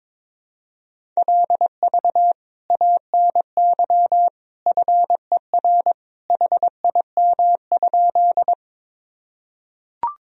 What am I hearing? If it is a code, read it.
Morse code, heard as a keyed tone, about 22 wpm, 700 Hz. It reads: LV ANY FER HIM?
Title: Sentences from Top 400 Words and Abbreviations 22wpm